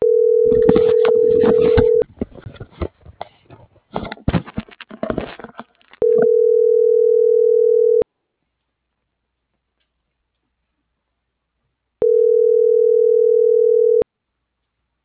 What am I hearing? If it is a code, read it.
Ringback tone (North America), 440 + 480 Hz.